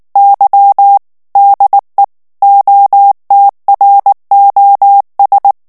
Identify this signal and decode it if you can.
Morse code, heard as a keyed tone, about 19 wpm, 800 Hz. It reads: Y DE OTROS
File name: morse mp3 / ESPAGNOL TEXTE EN CLAIR 2 VITESSE 18 MOTS MINUTES